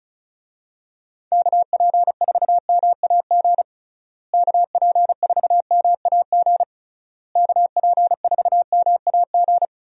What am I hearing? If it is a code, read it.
Morse code, heard as a keyed tone, about 36 wpm, 700 Hz. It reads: KP4MAG KP4MAG KP4MAG
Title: USA Call Signs - 6 Characters - Repeated 3x - 35wpm